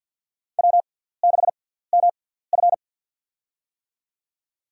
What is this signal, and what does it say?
Morse code, heard as a keyed tone, about 51 wpm, spaced as for 17 wpm, 700 Hz. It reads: W6MF
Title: USA Call Signs - 4 Characters - 6x Character Spacing - 50wpm